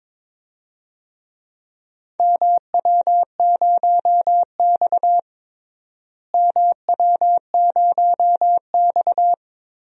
Morse code, heard as a keyed tone, about 22 wpm, 700 Hz. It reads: MW0X MW0X
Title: DXCC-100 Call Signs - 4 Characters - Repeated 2x - 22wpm